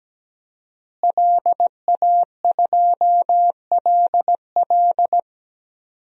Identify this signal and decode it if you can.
Morse code, heard as a keyed tone, about 17 wpm, 700 Hz. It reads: LA2LL
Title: DXCC-100 Call Signs - 5 Characters - Repeated 2x - 17wpm